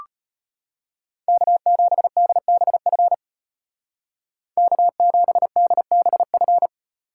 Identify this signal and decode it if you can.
Morse code, heard as a keyed tone, about 33 wpm, 700 Hz. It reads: K7DBF K7DBF